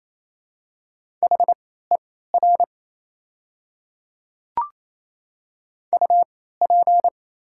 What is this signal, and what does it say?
Morse code, heard as a keyed tone, about 28 wpm, spaced as for 16 wpm, 700 Hz. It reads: HER UP
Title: 3x Character Spacing - Top 300 Words - 28wpm